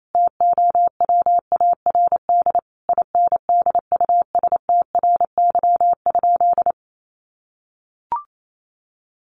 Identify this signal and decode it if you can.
Morse code, heard as a keyed tone, about 28 wpm, 700 Hz. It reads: TOWARD INDUSTRY?
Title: Sentences from Top 1300 Words 28wpm